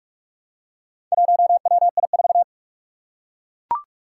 Morse code, heard as a keyed tone, about 41 wpm, 700 Hz. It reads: 1WI4
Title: Four Challenging Letter-Number Combinations 45wpm